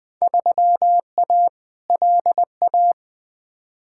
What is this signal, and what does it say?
Morse code, heard as a keyed tone, about 20 wpm, 700 Hz. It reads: 3A LA
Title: Field Day Exchanges - 20wpm